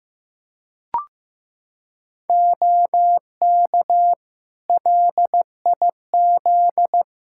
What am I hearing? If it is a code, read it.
Morse code, heard as a keyed tone, about 16 wpm, 700 Hz. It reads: OK LIZ
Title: K1USN SST QSO Practice 15wpm